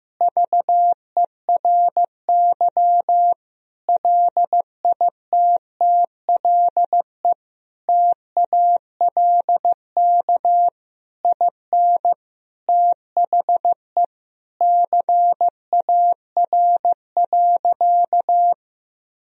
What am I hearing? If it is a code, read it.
Morse code, heard as a keyed tone, about 15 wpm, 700 Hz. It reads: VERY LITTLE TALK IN THE CAR.
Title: Sentences from Top 1000 Words - Review Part 1 - 15wpm